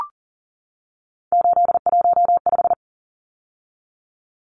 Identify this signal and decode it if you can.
Morse code, heard as a keyed tone, about 40 wpm, 700 Hz. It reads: 815
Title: Numbers - 3-Digit 40wpm